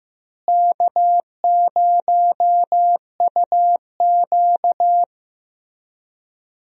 Morse code, heard as a keyed tone, about 16 wpm, 700 Hz. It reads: K0UQ
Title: USA Call Signs - 4 Characters - Repeated 3x - 15wpm